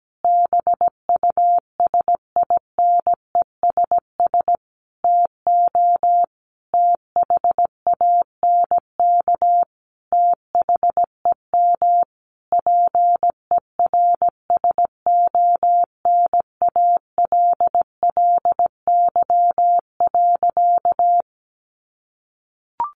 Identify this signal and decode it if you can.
Morse code, heard as a keyed tone, about 17 wpm, 700 Hz. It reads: BUSINESS TO THANK THEM PERSONALLY.
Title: Sentences from Top 1400 Words 17wpm